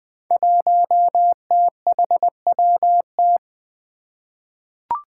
Morse code, heard as a keyed tone, about 20 wpm, 700 Hz. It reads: 1THWT
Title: Five Challenging Letter-Number Combinations 20wpm